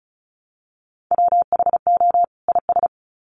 Morse code, heard as a keyed tone, about 34 wpm, 700 Hz. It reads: WHO IS
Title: Sets of 2 Words 35wpm